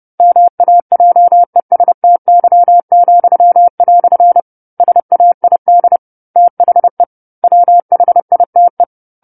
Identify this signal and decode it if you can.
Morse code, heard as a keyed tone, about 30 wpm, 700 Hz. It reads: MAJESTY," SAID THE WHITE
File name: 10-Chapter-10 30 WPM